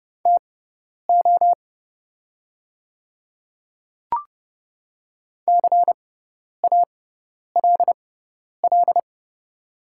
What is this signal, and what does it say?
Morse code, heard as a keyed tone, about 30 wpm, spaced as for 10 wpm, 700 Hz. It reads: TO CALL
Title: 6x Character Spacing - Top 100 Words - 30wpm